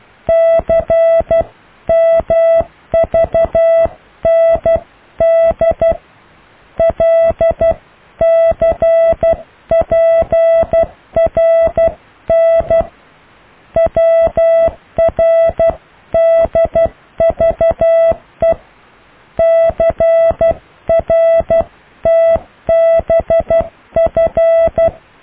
Morse code, heard as a keyed tone, about 12 wpm, 660 Hz. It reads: CMVND LCPRN WRDVE CRTBF
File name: cw / IJLUM+CDRO+EKTPV+BFHNW